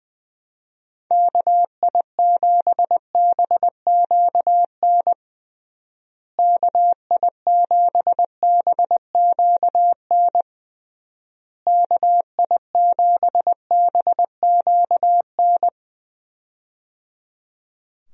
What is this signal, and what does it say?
Morse code, heard as a keyed tone, about 20 wpm, 700 Hz. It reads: KI7BQN KI7BQN KI7BQN
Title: USA Call Signs - 6 Characters - Repeated 3x - 20wpm